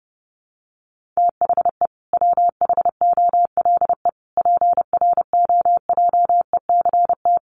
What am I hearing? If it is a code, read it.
Morse code, heard as a keyed tone, about 30 wpm, 700 Hz. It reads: THE WHOLE PROJECT
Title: Sentences from Top 1400 Words 30wpm